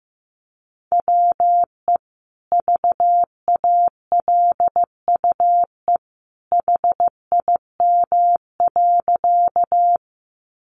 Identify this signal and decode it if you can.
Morse code, heard as a keyed tone, about 15 wpm, 700 Hz. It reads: WE VALUE HIM.
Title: Sentences from Top 600 Words 15wpm